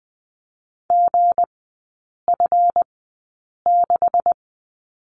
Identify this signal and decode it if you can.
Morse code, heard as a keyed tone, about 20 wpm, spaced as for 8 wpm, 700 Hz. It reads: GF6
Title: 198 Lesson 32e - Letter k - Three Characters 20wpm